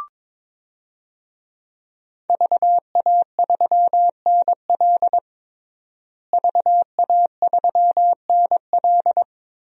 Morse code, heard as a keyed tone, about 22 wpm, 700 Hz. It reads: VA3NL VA3NL
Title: DXCC-100 Call Signs - 5 Characters - Repeated 3x - 22wpm